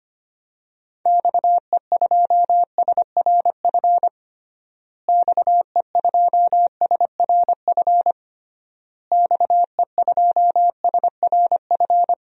Morse code, heard as a keyed tone, about 25 wpm, 700 Hz. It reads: XE2SRF XE2SRF XE2SRF
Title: DXCC-100 Call Signs - 6 Characters - Repeated 3x - 25wpm